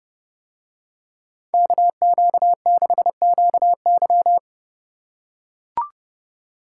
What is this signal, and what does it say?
Morse code, heard as a keyed tone, about 29 wpm, 700 Hz. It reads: KQ6QY